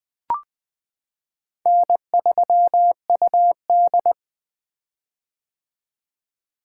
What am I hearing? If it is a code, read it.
Morse code, heard as a keyed tone, about 20 wpm, 700 Hz. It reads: N3UD
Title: 2023 POTA Activator Call Signs - 20wpm